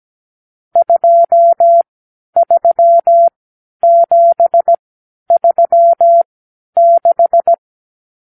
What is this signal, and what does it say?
Morse code, heard as a keyed tone, about 17 wpm, spaced as for 10 wpm, 680 Hz. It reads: 23736
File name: k7qo / Lesson 52 (8)